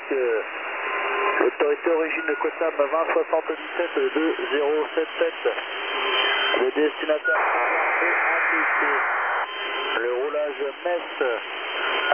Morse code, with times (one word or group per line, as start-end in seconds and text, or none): none